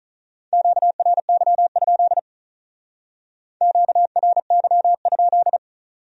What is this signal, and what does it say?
Morse code, heard as a keyed tone, about 47 wpm, 700 Hz. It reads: QRY? QRY?